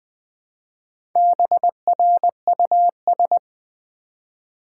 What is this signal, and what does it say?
Morse code, heard as a keyed tone, about 20 wpm, 700 Hz. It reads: BRUS
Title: Build A Word - 1 Letter At A Time - 5-Letter Words - 20wpm